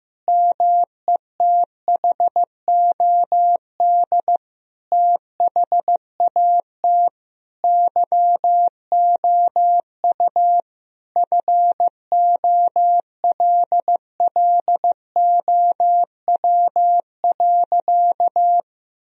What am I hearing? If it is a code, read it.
Morse code, heard as a keyed tone, about 15 wpm, 700 Hz. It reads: METHOD THAT YOU FOLLOW.
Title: Sentences from Top 1000 Words - Review Part 3 - 15wpm